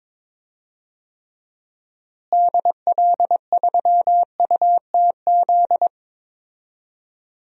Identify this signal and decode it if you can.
Morse code, heard as a keyed tone, about 22 wpm, 700 Hz. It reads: DL3UTZ
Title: DXCC-100 Call Signs - 6 Characters - Repeated 3x - 22wpm